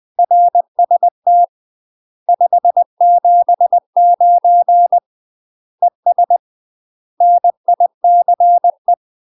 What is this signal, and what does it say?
Morse code, heard as a keyed tone, about 20 wpm, 700 Hz. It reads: RST 579 ES NICE
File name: UK MASTER QSOs TRAINING / 20WPM / UK MASTER QSOs 0038